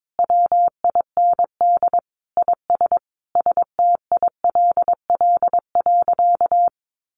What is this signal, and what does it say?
Morse code, heard as a keyed tone, about 22 wpm, 700 Hz. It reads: WIND IS STILL.
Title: Sentences from Top 1200 Words 22wpm